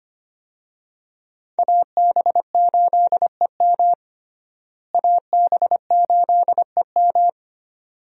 Morse code, heard as a keyed tone, about 25 wpm, 700 Hz. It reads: AB8EM AB8EM
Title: USA Call Signs - 5 Characters - Repeated 2x - 25wpm